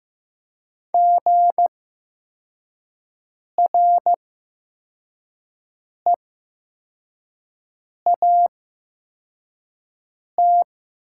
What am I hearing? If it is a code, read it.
Morse code, heard as a keyed tone, about 15 wpm, spaced as for 4 wpm, 700 Hz. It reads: GREAT
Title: 8x Character Spacing - Top 300 Words - 15wpm